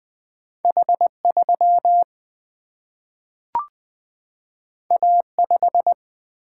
Morse code, heard as a keyed tone, about 20 wpm, 700 Hz. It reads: H3 A5
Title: Single-Letter Single-Number 20wpm